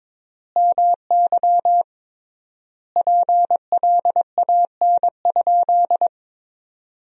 Morse code, heard as a keyed tone, about 22 wpm, 700 Hz. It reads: MY PLAN?
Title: 3x Word-Spacing - Sentences from Top 500 Words - 22wpm